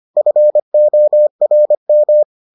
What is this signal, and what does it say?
Morse code, heard as a keyed tone, about 25 wpm, 600 Hz. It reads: FORM